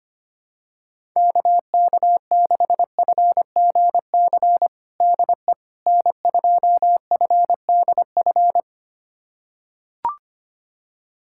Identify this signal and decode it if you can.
Morse code, heard as a keyed tone, about 25 wpm, 700 Hz. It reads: KK6FGC DE N2FDF
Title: USA Callsign DE Callsign 25wpm